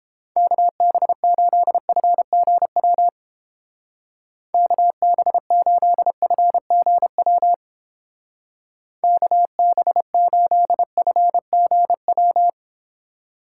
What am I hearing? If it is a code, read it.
Morse code, heard as a keyed tone, about 29 wpm, 700 Hz. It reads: KB8FGW KB8FGW KB8FGW